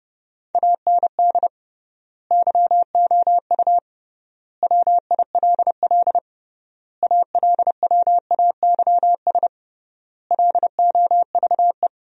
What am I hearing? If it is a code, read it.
Morse code, heard as a keyed tone, about 32 wpm, 700 Hz. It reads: AND YOU WILL ALWAYS LOVE
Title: 3x Word-Spacing - Sentences from Top 400 Words - 30wpm